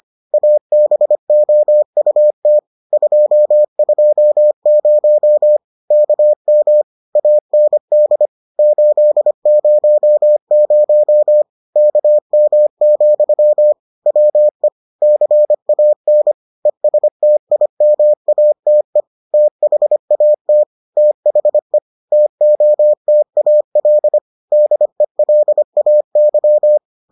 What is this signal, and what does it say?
Morse code, heard as a keyed tone, about 25 wpm, 600 Hz. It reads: ABOUT 220 KM AND 800 KM, WE CAN ESTIMATE THAT THE TOTAL DELAY